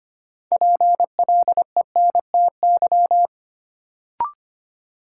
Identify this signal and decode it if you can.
Morse code, heard as a keyed tone, about 25 wpm, 700 Hz. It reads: PLENTY
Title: Sentences from Top 1300 Words - New Word Drill Alt - 25wpm